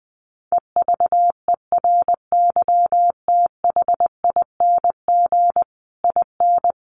Morse code, heard as a keyed tone, about 20 wpm, 700 Hz. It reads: EVERYTHING IN
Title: A Sound Of Thunder 20wpm